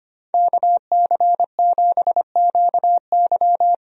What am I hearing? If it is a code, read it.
Morse code, heard as a keyed tone, about 25 wpm, 700 Hz. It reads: KC7QY
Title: USA Call Signs - 5 Characters - Repeated 2x - 25wpm